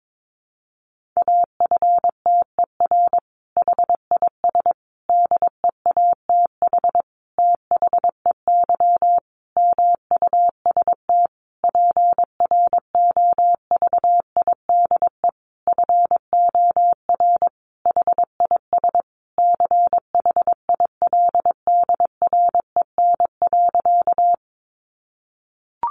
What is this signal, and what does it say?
Morse code, heard as a keyed tone, about 22 wpm, 700 Hz. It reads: AFTER HIS DEATH THEY MUST PROVIDE FOR HIS CHILDREN.
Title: Sentences from Top 1100 Words 22wpm